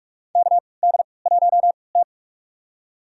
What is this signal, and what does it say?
Morse code, heard as a keyed tone, about 44 wpm, spaced as for 25 wpm, 700 Hz. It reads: KD1T
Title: USA Call Signs - 4 Characters - 3x Character Spacing - 45wpm